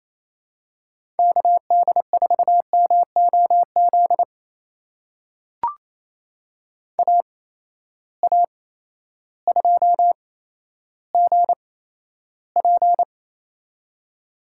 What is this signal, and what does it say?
Morse code, heard as a keyed tone, about 28 wpm, spaced as for 13 wpm, 700 Hz. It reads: KD4MOZ AA2GP
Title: USA Call Signs - 8x Character Spacing - 28wpm